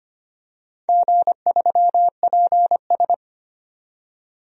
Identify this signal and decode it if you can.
Morse code, heard as a keyed tone, about 25 wpm, 700 Hz. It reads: G3PS
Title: DXCC-100 Call Signs - 4 Characters - 25wpm